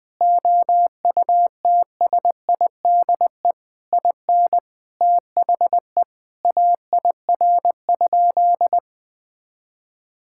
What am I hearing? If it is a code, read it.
Morse code, heard as a keyed tone, about 20 wpm, 700 Hz. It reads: OUTSIDE IN THE AIR?
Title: Sentences from Top 1000 Words - Review Part 6 - 20wpm